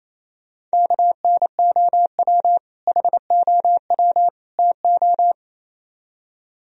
Morse code, heard as a keyed tone, about 35 wpm, 700 Hz. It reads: KNOW HOW TO